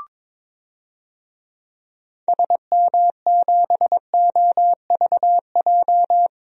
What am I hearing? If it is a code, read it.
Morse code, heard as a keyed tone, about 21 wpm, 700 Hz. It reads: SM7OVJ
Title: DXCC-100 Call Signs - 6 Characters - 22wpm